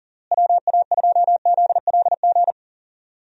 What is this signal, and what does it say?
Morse code, heard as a keyed tone, about 40 wpm, 700 Hz. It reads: WA2ZLG
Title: USA Call Signs - 6 Characters - Repeated 3x - 40wpm